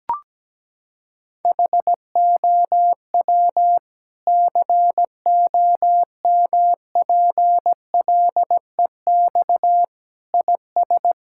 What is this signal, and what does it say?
Morse code, heard as a keyed tone, about 17 wpm, 700 Hz. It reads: HOW COMPLEX IS
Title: Sentences from Top 1000 Words - Review Part 6 - 17wpm